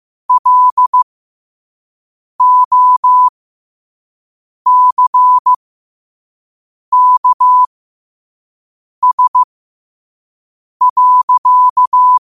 Morse code, heard as a keyed tone, about 15 wpm, spaced as for 5 wpm, 1000 Hz. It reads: LOCKS.
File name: qotdfiles / 5wpm